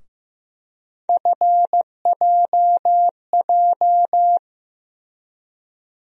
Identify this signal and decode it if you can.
Morse code, heard as a keyed tone, about 15 wpm, 700 Hz. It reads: FJJ
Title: USA Call Sign Postfixes - 3 Letters 15wpm